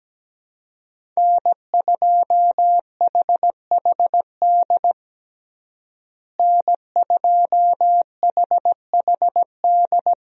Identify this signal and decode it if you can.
Morse code, heard as a keyed tone, about 17 wpm, 700 Hz. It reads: N2HHD N2HHD
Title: USA Call Signs - 5 Characters - Repeated 2x - 17wpm